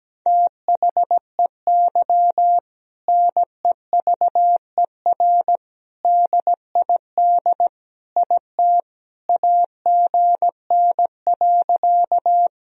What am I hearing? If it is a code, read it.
Morse code, heard as a keyed tone, about 17 wpm, 700 Hz. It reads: THEY NEVER DID IT AGN.
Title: Sentences from Top 300 Words and Abbreviations 17wpm